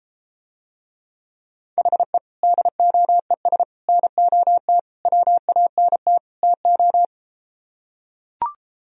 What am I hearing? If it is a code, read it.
Morse code, heard as a keyed tone, about 34 wpm, 700 Hz. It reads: HE DOES NOT WANT TO